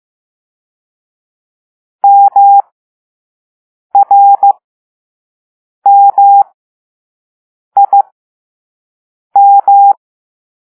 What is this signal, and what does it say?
Morse code, heard as a keyed tone, about 15 wpm, spaced as for 5 wpm, 800 Hz. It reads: MRMIM
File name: Morse / Level2B